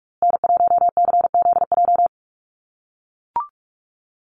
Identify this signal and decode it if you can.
Morse code, heard as a keyed tone, about 44 wpm, 700 Hz. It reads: N1CZJ
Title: USA Call Signs Repeated 3x 45wpm